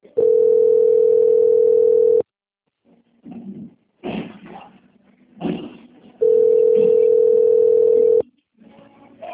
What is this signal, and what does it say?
Ringback tone (North America), 440 + 480 Hz.
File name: out- - - - - .5377